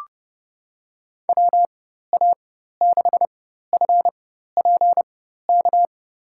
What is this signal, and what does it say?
Morse code, heard as a keyed tone, about 31 wpm, spaced as for 14 wpm, 700 Hz. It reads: WA6FPK
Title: USA Call Signs - 6 Characters - 4x Character Spacing - 30wpm